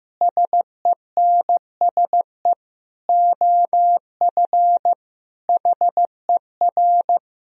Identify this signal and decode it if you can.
Morse code, heard as a keyed tone, about 16 wpm, 700 Hz. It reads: SENSE OF HER
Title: Sentences from Top 1000 Words - Review Part 4 - 15wpm